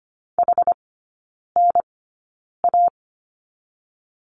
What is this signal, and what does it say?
Morse code, heard as a keyed tone, about 25 wpm, spaced as for 9 wpm, 690 Hz. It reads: HNA
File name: Lesson 14e - Number 2 - Three Characters 25wpm